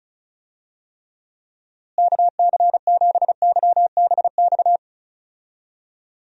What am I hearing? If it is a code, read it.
Morse code, heard as a keyed tone, about 34 wpm, 700 Hz. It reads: KC7YBX
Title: USA Call Signs - 6 Characters - Repeated 3x - 35wpm